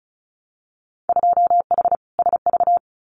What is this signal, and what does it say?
Morse code, heard as a keyed tone, about 36 wpm, 700 Hz. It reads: 2H SV
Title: Winter Field Day Exchanges Encore 35wpm